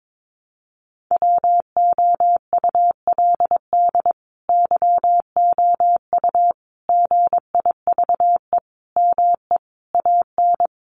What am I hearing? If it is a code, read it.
Morse code, heard as a keyed tone, about 22 wpm, 700 Hz. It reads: WOULD YOU GIVE ME AN
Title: Sentences from Top 1000 Words 22wpm